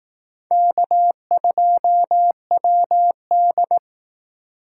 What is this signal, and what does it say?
Morse code, heard as a keyed tone, about 18 wpm, 700 Hz. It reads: K2WD
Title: Speed Racing - USA 1x2 Call Signs 15wpm